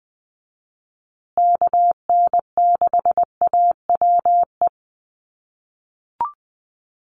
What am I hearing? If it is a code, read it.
Morse code, heard as a keyed tone, about 20 wpm, 700 Hz. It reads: KN6AWE
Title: USA Call Signs Repeated 3x 20wpm